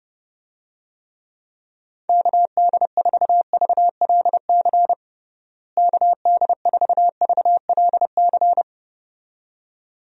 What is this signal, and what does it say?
Morse code, heard as a keyed tone, about 31 wpm, 700 Hz. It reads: KD4VLC KD4VLC
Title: USA Call Signs - 6 Characters - Repeated 2x - 30wpm